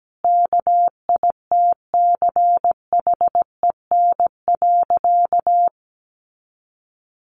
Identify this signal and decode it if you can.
Morse code, heard as a keyed tone, about 17 wpm, 700 Hz. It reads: KITCHEN.
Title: Sentences from Top 1400 Words 17wpm